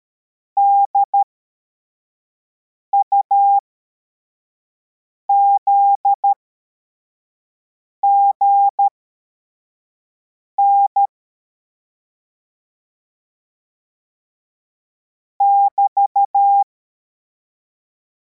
Morse code, heard as a keyed tone, about 13 wpm, spaced as for 4 wpm, 800 Hz. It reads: DUZGN <BT>
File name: formation / cw / Text53